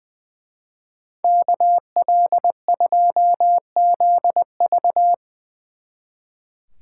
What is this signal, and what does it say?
Morse code, heard as a keyed tone, about 23 wpm, 700 Hz. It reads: KL2ZV
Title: Speed Racing - USA Call Signs 20wpm